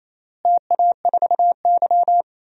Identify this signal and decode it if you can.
Morse code, heard as a keyed tone, about 28 wpm, 700 Hz. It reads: TA4Y
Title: DXCC-100 Call Signs - 4 Characters - Repeated 3x - 28wpm